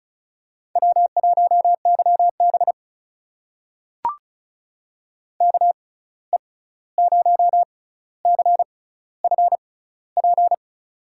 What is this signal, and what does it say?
Morse code, heard as a keyed tone, about 36 wpm, spaced as for 16 wpm, 700 Hz. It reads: W1YB KE0CFP
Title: USA Call Signs - 6x Character Spacing - 35wpm